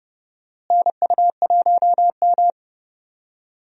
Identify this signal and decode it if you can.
Morse code, heard as a keyed tone, about 30 wpm, 700 Hz. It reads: NU1M
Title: USA Call Signs - 4 Characters - Repeated 3x - 30wpm